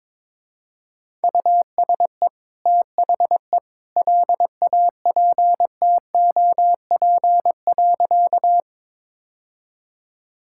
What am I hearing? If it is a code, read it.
Morse code, heard as a keyed tone, about 22 wpm, 700 Hz. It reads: USE THE LAPTOP.
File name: practice / 22wpm / Easy Sentences with Hints - 22wpm